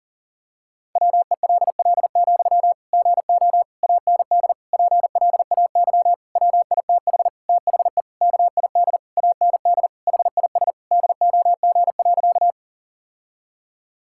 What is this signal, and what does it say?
Morse code, heard as a keyed tone, about 40 wpm, 700 Hz. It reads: WELL, GO AND PLAY WITH THE KID AND HIS DOG.
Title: Sentences from Top 1600 Words 40wpm